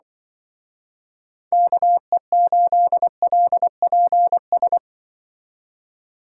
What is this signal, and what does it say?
Morse code, heard as a keyed tone, about 23 wpm, 700 Hz. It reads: KE8LPS
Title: Speed Racing - USA Call Signs 20wpm